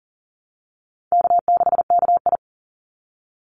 Morse code, heard as a keyed tone, about 40 wpm, 700 Hz. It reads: K6KI
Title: Call Signs - 40wpm